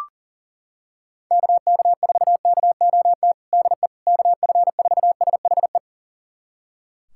Morse code, heard as a keyed tone, about 40 wpm, 700 Hz. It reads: KK4KOT DE KF4SHE
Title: USA Callsign DE Callsign 40wpm